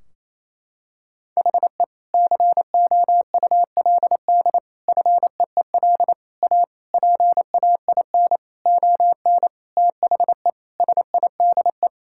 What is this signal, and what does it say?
Morse code, heard as a keyed tone, about 28 wpm, 700 Hz. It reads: HE COULD FEEL A PAIN ON THE SIDE
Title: Sentences from Top 1600 Words 28wpm